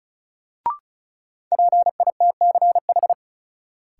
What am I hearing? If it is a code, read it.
Morse code, heard as a keyed tone, about 36 wpm, 700 Hz. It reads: PITCH
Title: Sentences from Top 1300 Words - New Word Drill - 35wpm